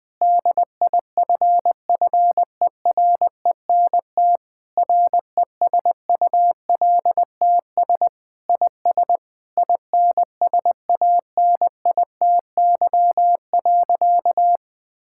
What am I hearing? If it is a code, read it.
Morse code, heard as a keyed tone, about 21 wpm, 700 Hz. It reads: DIFFERENT RESULTS IS INSANITY.
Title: Sentences with 9-Word Phrases - Spoken Beginning - 20wpm